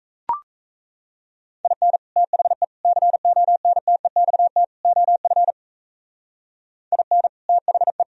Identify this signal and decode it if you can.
Morse code, heard as a keyed tone, about 35 wpm, 700 Hz. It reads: IN THE CONTEXT OF IN THE
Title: Speed-Racing - Sets of 4 Words 28wpm